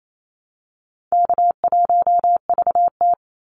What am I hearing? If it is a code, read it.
Morse code, heard as a keyed tone, about 28 wpm, 700 Hz. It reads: K1VT
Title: Call Signs - 28wpm